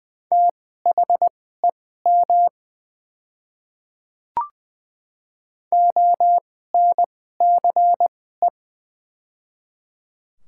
Morse code, heard as a keyed tone, about 20 wpm, spaced as for 14 wpm, 700 Hz. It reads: THEM ONCE
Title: 2x Character Spacing - Top 500 Words - 20wpm